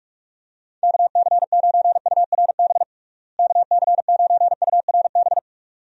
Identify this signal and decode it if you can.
Morse code, heard as a keyed tone, about 46 wpm, 700 Hz. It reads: KC9URB KC9URB
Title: USA Call Signs - 6 Characters - Repeated 2x - 45wpm